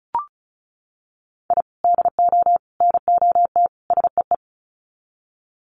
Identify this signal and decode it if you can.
Morse code, heard as a keyed tone, about 34 wpm, 700 Hz. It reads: I DO NOT SEE
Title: Sets of 4 Words 35wpm